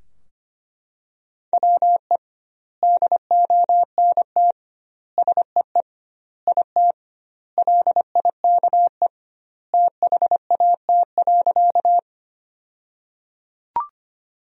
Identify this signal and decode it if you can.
Morse code, heard as a keyed tone, about 25 wpm, 700 Hz. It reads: WE DONT SEE IT LIKE THAT.
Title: 2x Word-Spacing - Sentences from Top 300 Words - 25wpm